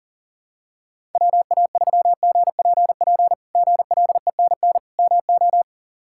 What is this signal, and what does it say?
Morse code, heard as a keyed tone, about 40 wpm, 700 Hz. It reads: WA3GPP GLENN MO
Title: K1USN SST QSO Practice - 40wpm